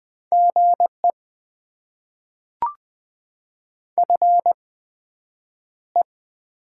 Morse code, heard as a keyed tone, about 20 wpm, spaced as for 9 wpm, 700 Hz. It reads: GE FE
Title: Top 20 QSO Elements - 8x Character Spacing - 20wpm